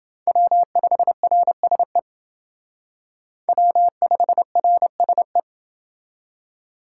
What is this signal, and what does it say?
Morse code, heard as a keyed tone, about 28 wpm, 700 Hz. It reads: W5RSE W5RSE